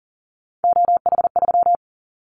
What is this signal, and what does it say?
Morse code, heard as a keyed tone, about 41 wpm, 700 Hz. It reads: OH3
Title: Callsign Prefixes for Top 50 DXCC - 40wpm